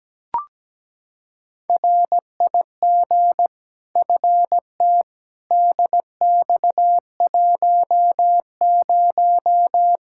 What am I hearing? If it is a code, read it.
Morse code, heard as a keyed tone, about 17 wpm, 700 Hz. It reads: RIG FT DX10
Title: K1USN SST QSO Practice 17wpm